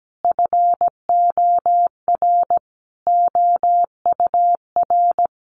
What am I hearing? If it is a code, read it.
Morse code, heard as a keyed tone, about 17 wpm, 700 Hz. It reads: FOR OUR
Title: Sentences from Top 1200 Words 17wpm